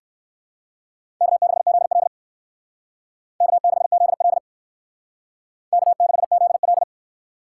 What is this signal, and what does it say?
Morse code, heard as a keyed tone, about 57 wpm, 700 Hz. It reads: K6ZL K6ZL K6ZL